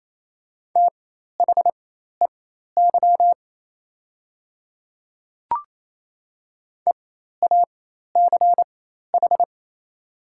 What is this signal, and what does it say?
Morse code, heard as a keyed tone, about 28 wpm, spaced as for 13 wpm, 690 Hz. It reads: THEY EACH
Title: 4x Character Spacing - Top 200 Words - 28wpm